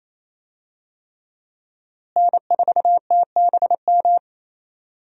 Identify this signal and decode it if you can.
Morse code, heard as a keyed tone, about 27 wpm, 700 Hz. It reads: N4TBM
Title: USA Call Signs - 5 Characters - Repeated 3x - 28wpm